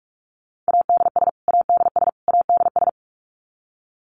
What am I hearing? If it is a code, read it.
Morse code, heard as a keyed tone, about 44 wpm, 700 Hz. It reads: ADS ADS ADS
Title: Abbreviations Repeated 3x - 45wpm